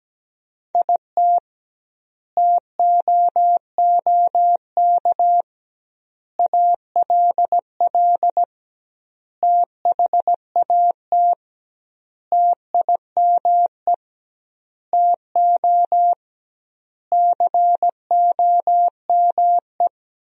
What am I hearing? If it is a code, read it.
Morse code, heard as a keyed tone, about 17 wpm, 700 Hz. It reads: IT TOOK ALL THAT TIME TO COME
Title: 2x Word-Spacing - Sentences from Top 300 Words - 17wpm